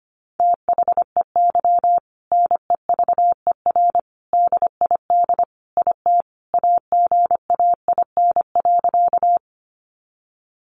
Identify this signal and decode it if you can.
Morse code, heard as a keyed tone, about 25 wpm, 700 Hz. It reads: THEY NEVER DID IT AGAIN.
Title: Sentences from Top 300 Words 25wpm